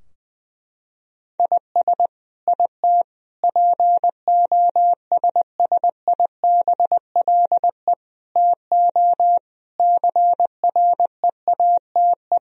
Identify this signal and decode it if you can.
Morse code, heard as a keyed tone, about 20 wpm, 700 Hz. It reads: IS IT POSSIBLE TO CREATE
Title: Sentences from Top 1000 Words - Review Part 3 - 20wpm